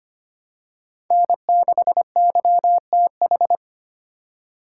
Morse code, heard as a keyed tone, about 25 wpm, 700 Hz. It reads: N6YTH
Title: USA Call Signs - 5 Characters - 4x Character Spacing - 25wpm